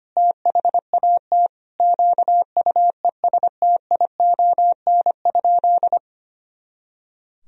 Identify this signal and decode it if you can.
Morse code, heard as a keyed tone, about 25 wpm, 700 Hz. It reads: THAT QUESTION?
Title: Sentences from Top 400 Words and Abbreviations 25wpm